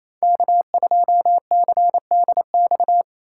Morse code, heard as a keyed tone, about 27 wpm, 700 Hz. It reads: K2CDX
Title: K1USN SST QSO Practice - 28wpm